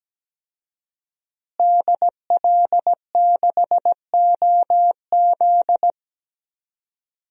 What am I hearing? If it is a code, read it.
Morse code, heard as a keyed tone, about 17 wpm, 700 Hz. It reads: DL6OZ
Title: DXCC-100 Call Signs - 5 Characters - Repeated 2x - 17wpm